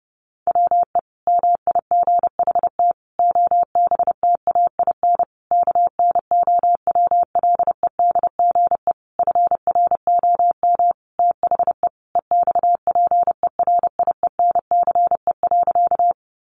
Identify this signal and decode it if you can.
Morse code, heard as a keyed tone, about 30 wpm, 700 Hz. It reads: WE MIGHT OBTAIN KNOWLEDGE FROM THE EXPERIENCE.
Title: Sentences from Top 1300 Words 30wpm